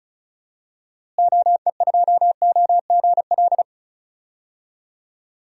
Morse code, heard as a keyed tone, about 34 wpm, 700 Hz. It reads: OE2OGL
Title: DXCC-100 Call Signs - 6 Characters - Repeated 3x - 35wpm